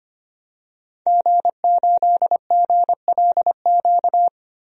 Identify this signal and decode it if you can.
Morse code, heard as a keyed tone, about 25 wpm, 700 Hz. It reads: G8GLQ
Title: DXCC-100 Call Signs - 5 Characters - 25wpm